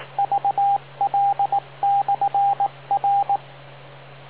Morse code, heard as a keyed tone, about 18 wpm, 800 Hz, with radio noise. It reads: VL/R